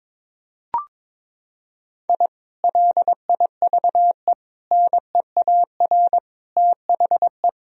Morse code, heard as a keyed tone, about 22 wpm, 700 Hz. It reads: I LIVE NEAR THE
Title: Sentences from Top 1000 Words - Review Part 3 - 22wpm